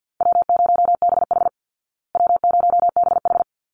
Morse code, heard as a keyed tone, about 50 wpm, 700 Hz. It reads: W0BH W0BH
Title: Easy Call Signs Repeated 2x - 50wpm